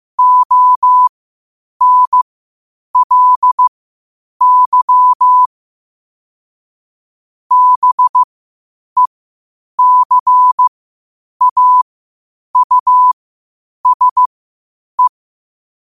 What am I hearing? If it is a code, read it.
Morse code, heard as a keyed tone, about 15 wpm, spaced as for 8 wpm, 1000 Hz. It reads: ONLY BECAUSE
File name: qotdfiles / 8wpm